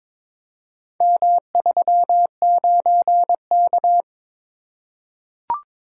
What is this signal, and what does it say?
Morse code, heard as a keyed tone, about 21 wpm, 700 Hz. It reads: M39K
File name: practice / 22wpm / Four Letter-Number Combinations - 22wpm